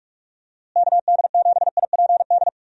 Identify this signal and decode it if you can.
Morse code, heard as a keyed tone, about 46 wpm, 700 Hz. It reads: KD7IPD
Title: USA Call Signs - 6 Characters - Repeated 2x - 45wpm